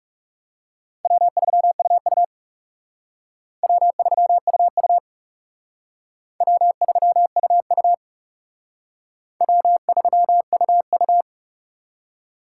Morse code, heard as a keyed tone, about 37 wpm, 700 Hz. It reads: W3UU W3UU W3UU W3UU
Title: Speed Racing - USA 1x2 Call Signs 30wpm